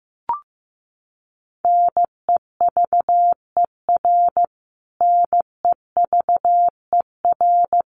Morse code, heard as a keyed tone, about 15 wpm, 700 Hz. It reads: NEVER NEVER
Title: Top 300 Words Repeated 15wpm